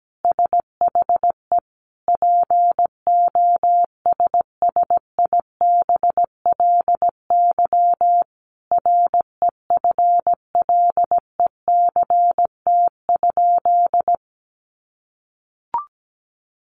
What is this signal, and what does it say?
Morse code, heard as a keyed tone, about 17 wpm, 700 Hz. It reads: SHE POSSIBLY REFLECT?
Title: Sentences from Top 1500 Words 17wpm